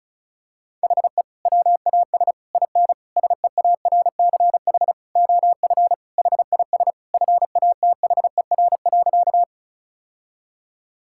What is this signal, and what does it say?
Morse code, heard as a keyed tone, about 36 wpm, 700 Hz. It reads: HE WAS IN SEARCH OF HIS FATHER.
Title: Sentences from Top 1000 Words - Review Part 5 - 35wpm